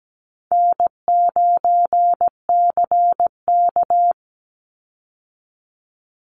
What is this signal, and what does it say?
Morse code, heard as a keyed tone, about 17 wpm, 700 Hz. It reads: N9CK
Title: Easy Call Signs Repeated 3x - 17wpm